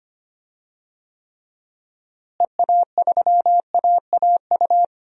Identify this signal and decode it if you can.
Morse code, heard as a keyed tone, about 25 wpm, 700 Hz. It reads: EA3AAU
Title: DXCC-100 Call Signs - 6 Characters - Repeated 3x - 25wpm